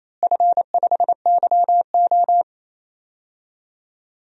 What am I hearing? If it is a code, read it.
Morse code, heard as a keyed tone, about 28 wpm, 700 Hz. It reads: F5YO
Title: DXCC-100 Call Signs - 4 Characters - Repeated 2x - 28wpm